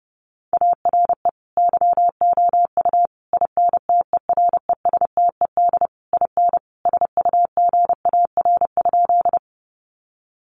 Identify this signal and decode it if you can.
Morse code, heard as a keyed tone, about 30 wpm, 700 Hz. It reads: ARE YOU INTERESTED IN SUGAR?
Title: Sentences from Top 1200 Words 30wpm